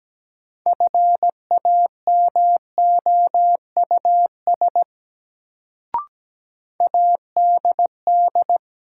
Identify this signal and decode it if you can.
Morse code, heard as a keyed tone, about 17 wpm, 700 Hz. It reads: FAMOUS ADD
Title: Sentences from Top 1500 Words - New Word Drill Alt - 17wpm